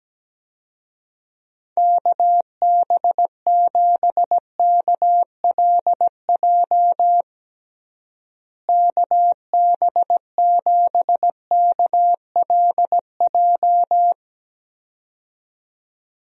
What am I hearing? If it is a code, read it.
Morse code, heard as a keyed tone, about 17 wpm, 700 Hz. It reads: KB7KLJ KB7KLJ
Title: USA Call Signs - 6 Characters - Repeated 2x - 17wpm